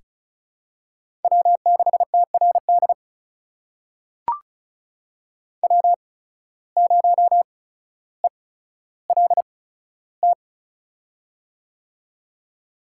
Morse code, heard as a keyed tone, about 36 wpm, spaced as for 15 wpm, 700 Hz. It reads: W6TRD W0ELT
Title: USA Call Signs - 8x Character Spacing - 35wpm